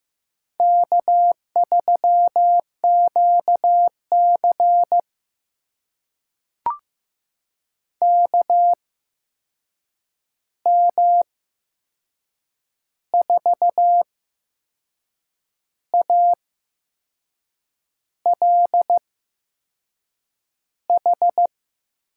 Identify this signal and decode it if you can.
Morse code, heard as a keyed tone, about 15 wpm, spaced as for 6 wpm, 700 Hz. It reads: K3QC KM4ALH
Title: USA Call Signs - 8x Character Spacing - 15wpm